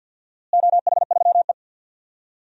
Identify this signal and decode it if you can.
Morse code, heard as a keyed tone, about 53 wpm, 700 Hz. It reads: OH3E
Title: DXCC-100 Call Signs - 4 Characters - Repeated 3x - 50wpm